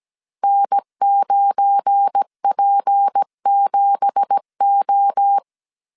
Morse code, heard as a keyed tone, about 17 wpm, 800 Hz. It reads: N9P7O
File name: CW / DET / 1 9 1500C S C 16wpm 2909